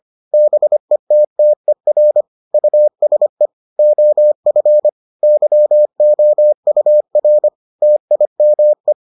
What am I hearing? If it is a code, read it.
Morse code, heard as a keyed tone, about 25 wpm, 600 Hz. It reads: BETTER USE OF YOUR TIME